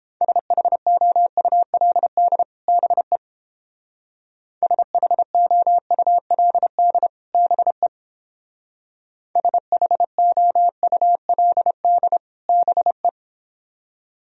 Morse code, heard as a keyed tone, about 30 wpm, 700 Hz. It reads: SHOULD BE SHOULD BE SHOULD BE